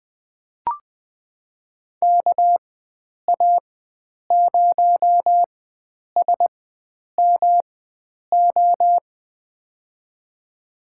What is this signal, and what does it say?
Morse code, heard as a keyed tone, about 20 wpm, spaced as for 9 wpm, 700 Hz. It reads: KA0SMO
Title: USA Call Signs - 6 Characters - 4x Character Spacing - 20wpm